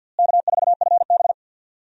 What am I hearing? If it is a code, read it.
Morse code, heard as a keyed tone, about 52 wpm, 700 Hz. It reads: K4FB
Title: USA Call Signs - 4 Characters - Repeated 3x - 50wpm